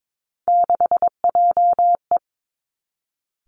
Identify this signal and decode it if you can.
Morse code, heard as a keyed tone, about 22 wpm, 700 Hz. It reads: 6JE
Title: Three Challenging Letter-Number Combinations 22wpm